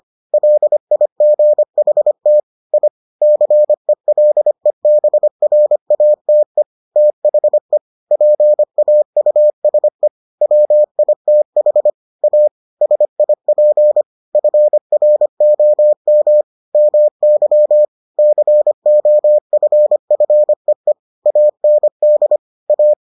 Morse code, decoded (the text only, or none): LIGHT I CELEBRATE THE PAUSE WITH A SIP FROM MY COFFEE AND A